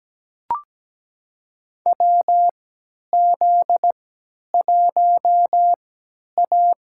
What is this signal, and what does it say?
Morse code, heard as a keyed tone, about 17 wpm, spaced as for 10 wpm, 700 Hz. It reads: WZ1A
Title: USA Call Signs - 4 Characters - 3x Character Spacing - 17wpm